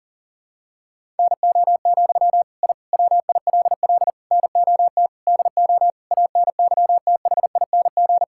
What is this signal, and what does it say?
Morse code, heard as a keyed tone, about 40 wpm, 700 Hz. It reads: NO, I WILL NOT DO ANYTHING
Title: Sentences from Top 1000 Words - Review Part 6 - 40wpm